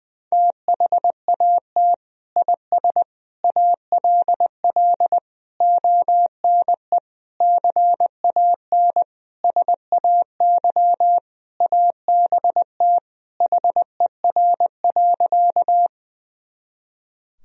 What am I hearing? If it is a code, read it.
Morse code, heard as a keyed tone, about 21 wpm, 700 Hz. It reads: THAT IS ALL ONE CAN SAY ABT HER.
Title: Sentences from Top 200 Words and Abbreviations 20wpm